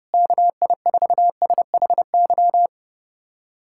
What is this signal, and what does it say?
Morse code, heard as a keyed tone, about 32 wpm, 700 Hz. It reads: KI4SHY
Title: USA Callsign DE Callsign - 6 Characters - 30wpm